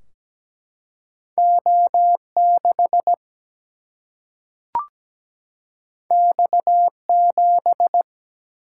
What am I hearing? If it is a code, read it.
Morse code, heard as a keyed tone, about 17 wpm, 700 Hz. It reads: O6 X7
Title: Single-Letter Single-Number 17wpm